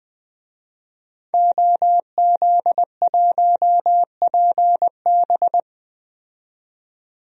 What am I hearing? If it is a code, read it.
Morse code, heard as a keyed tone, about 20 wpm, 700 Hz. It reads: OZ1PB
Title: DXCC-100 Call Signs - 5 Characters - 20wpm